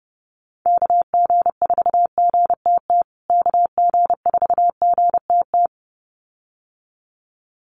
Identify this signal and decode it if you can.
Morse code, heard as a keyed tone, about 30 wpm, 700 Hz. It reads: KG4GTT KG4GTT
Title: USA Call Signs Repeated 2x 30wpm